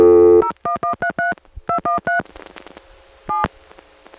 DTMF, heard as telephone touch tones, tones of about 120 ms. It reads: 01133 213 7